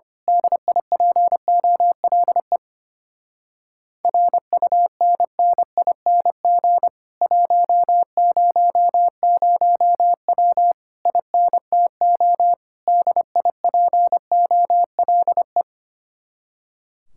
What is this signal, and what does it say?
Morse code, heard as a keyed tone, about 24 wpm, 700 Hz. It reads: DIPOLE RUNNING 100W INTO DIPOLE
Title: Speed-Racing Common QSO Phrases Encore - 25wpm